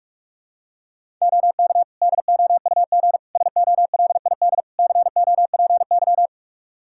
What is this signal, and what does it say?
Morse code, heard as a keyed tone, about 41 wpm, 700 Hz. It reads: OK DOUG SOLID COPY
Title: Practice QSO - Exchange 3 - More Info - 45wpm